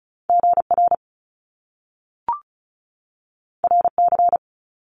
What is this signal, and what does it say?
Morse code, heard as a keyed tone, about 36 wpm, 700 Hz. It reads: GR RC
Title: Top Two Letter Combinations in Words 35wpm